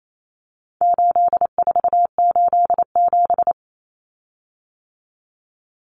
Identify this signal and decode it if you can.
Morse code, heard as a keyed tone, about 28 wpm, 700 Hz. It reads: 8487
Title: Common Cut Numbers - 4-Digit 28wpm